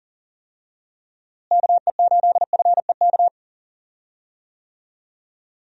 Morse code, heard as a keyed tone, about 40 wpm, 700 Hz. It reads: KE8FEK
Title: USA Call Signs - 6 Characters - Repeated 3x - 40wpm